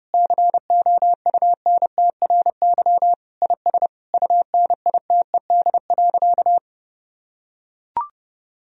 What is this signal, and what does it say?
Morse code, heard as a keyed tone, about 32 wpm, 700 Hz. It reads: COUNTRY IS UNITED.
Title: Sentences from Top 1000 Words - Review Part 4 - 30wpm